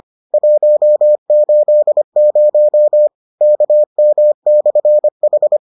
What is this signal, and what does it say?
Morse code, heard as a keyed tone, about 25 wpm, 600 Hz. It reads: 180 KM/H